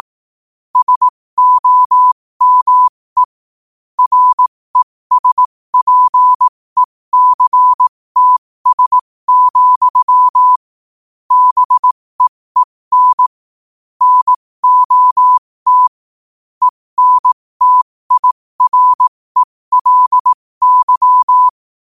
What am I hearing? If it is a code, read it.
Morse code, heard as a keyed tone, about 18 wpm, spaced as for 15 wpm, 1000 Hz. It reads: SOME RESPECTS, BEEN NOT ENTIRELY
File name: qotdfiles / 15wpm